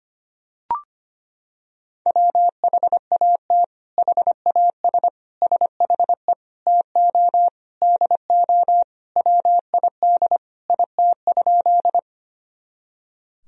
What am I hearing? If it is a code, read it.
Morse code, heard as a keyed tone, about 25 wpm, 700 Hz. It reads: WHAT HAS SHE TO DO WID IT?
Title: Sentences from Top 100 Words and Abbreviations 25wpm